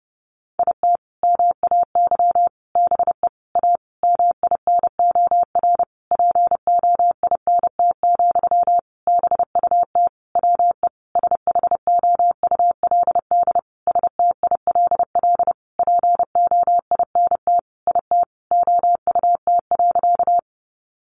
Morse code, heard as a keyed tone, about 30 wpm, 700 Hz. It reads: IT MAY BE A MINOR POINT, BUT WE SHOULD STILL POINT IT OUT.
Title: Sentences from Top 1500 Words 30wpm